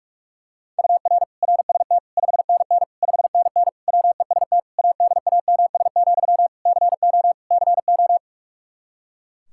Morse code, heard as a keyed tone, about 42 wpm, 700 Hz. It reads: UR RST 5NN 5NN WEST ADAMS, CO CO
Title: Practice QSO - Exchange 1 - Beginning - 45wpm